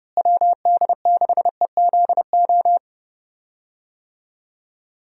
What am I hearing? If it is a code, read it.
Morse code, heard as a keyed tone, about 30 wpm, 700 Hz. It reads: WD6EZO
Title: USA Call Signs - 6 Characters - Repeated 3x - 30wpm